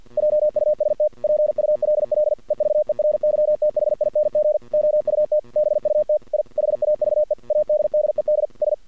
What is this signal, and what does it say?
Morse code, heard as a keyed tone, about 48 wpm, 600 Hz. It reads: JUST PULL EVERYTHING OUT BUT I HAVE ADDED S
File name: archived / CRSnet